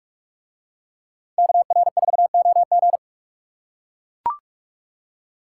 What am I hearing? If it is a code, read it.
Morse code, heard as a keyed tone, about 40 wpm, 700 Hz. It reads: KR4OG